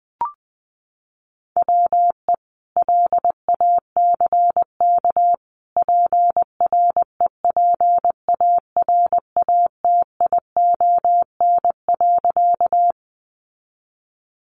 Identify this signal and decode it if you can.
Morse code, heard as a keyed tone, about 20 wpm, 700 Hz. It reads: WE LACK PREPARATION.
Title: Sentences from Top 1400 Words 20wpm